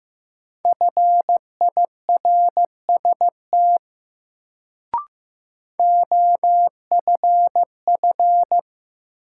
Morse code, heard as a keyed tone, about 15 wpm, 700 Hz. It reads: FIRST OFF
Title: Sets of 2 Words - New Word Drill - 15wpm